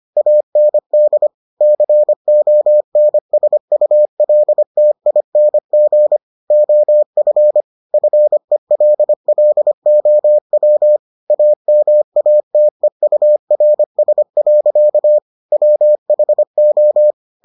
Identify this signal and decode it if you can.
Morse code, heard as a keyed tone, about 25 wpm, 600 Hz. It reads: AND CONSULTING OF FELLOW AMATEURS. WHO